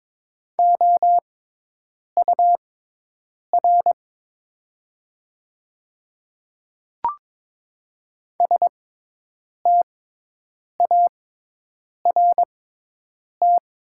Morse code, heard as a keyed tone, about 22 wpm, spaced as for 8 wpm, 700 Hz. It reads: OUR START
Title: 6x Character Spacing - Top 300 Words - 22wpm